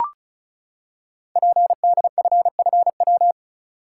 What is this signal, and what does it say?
Morse code, heard as a keyed tone, about 33 wpm, 700 Hz. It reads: PDFFW
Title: Code Groups - 5 Letter:Numbers - 35wpm